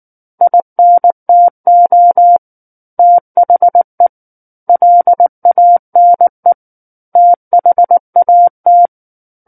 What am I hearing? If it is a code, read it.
Morse code, heard as a keyed tone, about 19 wpm, 700 Hz. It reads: INTO THE LANE THAT
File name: Chapter 10 - 19 WPM